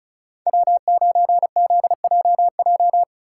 Morse code, heard as a keyed tone, about 36 wpm, 700 Hz. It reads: W9ZJJ
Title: USA Call Signs - 5 Characters - 4x Character Spacing - 35wpm